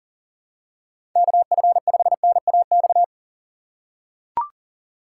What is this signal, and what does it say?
Morse code, heard as a keyed tone, about 40 wpm, 700 Hz. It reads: KF5NAX
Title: USA Call Signs - 6 Characters - 6x Character Spacing - 40wpm